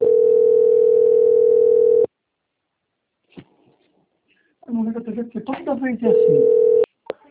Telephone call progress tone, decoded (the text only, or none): ringback tone